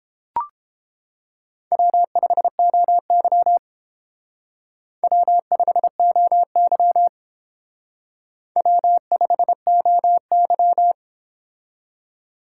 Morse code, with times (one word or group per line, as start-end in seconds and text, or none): 1.7-3.6 W5OY
5.0-7.1 W5OY
8.6-10.9 W5OY